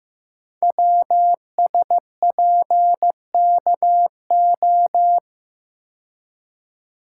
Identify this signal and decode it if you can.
Morse code, heard as a keyed tone, about 15 wpm, 700 Hz. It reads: WSPKO
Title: Code Groups - 5 Letters - 15wpm